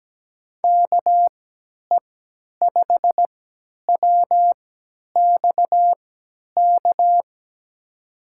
Morse code, heard as a keyed tone, about 17 wpm, spaced as for 10 wpm, 700 Hz. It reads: KE5WXK
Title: USA Call Signs - 3x Character Spacing - 17wpm